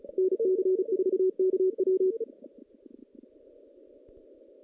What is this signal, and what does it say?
Morse code, heard as a keyed tone, about 33 wpm, 370 Hz. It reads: NC4KW
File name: 2016 JAN CW NAQP / 2016 Jan naqp k5ot - 0-5